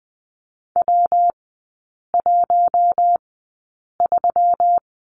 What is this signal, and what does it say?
Morse code, heard as a keyed tone, about 20 wpm, spaced as for 8 wpm, 700 Hz. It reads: W13
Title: 166 Lesson 27e - Number 7 - Three Characters 20wpm